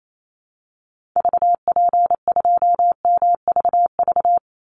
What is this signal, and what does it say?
Morse code, heard as a keyed tone, about 28 wpm, 700 Hz. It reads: VP2MVV
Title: Call Signs - 28wpm